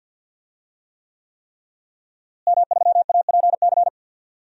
Morse code, heard as a keyed tone, about 52 wpm, 700 Hz. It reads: M3APC
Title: DXCC-100 Call Signs - 5 Characters - Repeated 3x - 50wpm